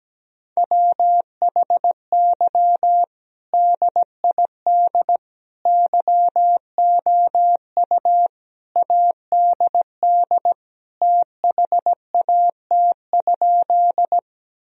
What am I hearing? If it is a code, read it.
Morse code, heard as a keyed tone, about 17 wpm, 700 Hz. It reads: WHY DID YOU ADD THAT?
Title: Sentences from Top 1000 Words - Review Part 1 - 17wpm